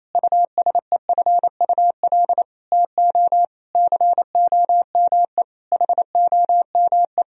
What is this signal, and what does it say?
Morse code, heard as a keyed tone, about 28 wpm, 700 Hz. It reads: USEFUL TO COME HOME
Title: Sentences from Top 1000 Words - Review Part 4 - 28wpm